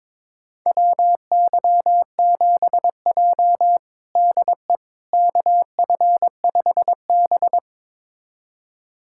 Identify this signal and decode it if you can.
Morse code, heard as a keyed tone, about 22 wpm, 700 Hz. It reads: WY7J DE KF5B
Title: USA Callsign DE Callsign - 4 Characters - 22wpm